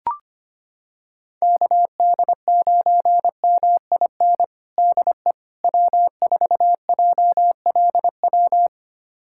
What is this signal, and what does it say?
Morse code, heard as a keyed tone, about 25 wpm, 700 Hz. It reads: KD9MIN DE W4JLW
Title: USA Callsign DE Callsign 25wpm